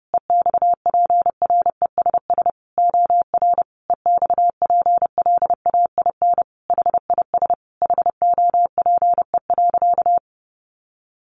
Morse code, heard as a keyed tone, about 28 wpm, 700 Hz. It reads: EXPRESS OR EXPLAIN HIS HOPE.
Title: Sentences from Top 1100 Words 30wpm